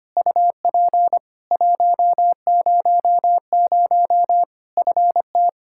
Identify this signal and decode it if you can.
Morse code, heard as a keyed tone, about 25 wpm, 700 Hz. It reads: UP 100 FT
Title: Practice QSO - Exchange 2 - Info - 25wpm